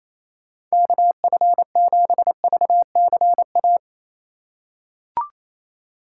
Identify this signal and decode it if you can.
Morse code, heard as a keyed tone, about 28 wpm, 700 Hz. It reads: KF7VCA
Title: USA Call Signs - 6 Characters - 3x Character Spacing - 28wpm